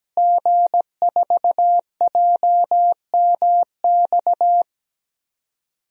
Morse code, heard as a keyed tone, about 17 wpm, 700 Hz. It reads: G4JMX
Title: DXCC-100 Call Signs - 5 Characters - Repeated 3x - 17wpm